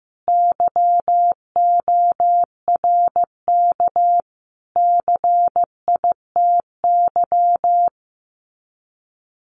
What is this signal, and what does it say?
Morse code, heard as a keyed tone, about 15 wpm, 700 Hz. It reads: YORK CITY
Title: Sets of 5 Words Encore2 15wpm